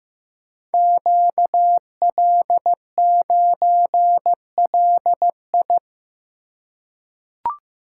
Morse code, heard as a keyed tone, about 15 wpm, 700 Hz. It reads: QL9LI
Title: Code Groups - 5 Letter:Numbers - 15wpm